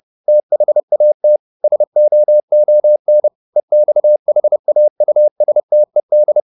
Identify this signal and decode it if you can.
Morse code, heard as a keyed tone, about 30 wpm, 600 Hz. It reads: THAT SOON EXHAUSTED
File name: the old man and the sea / the old man and the sea 1 30wpm